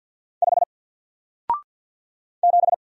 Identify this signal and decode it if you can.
Morse code, heard as a keyed tone, about 50 wpm, 700 Hz. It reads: 5 7
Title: Single Letter-Number Alt 50wpm